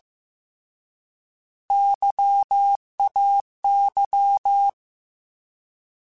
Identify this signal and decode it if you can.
Morse code, heard as a keyed tone, about 15 wpm, 800 Hz. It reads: YAY
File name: Alphabetical / Alphabetical mp3s / Ys @13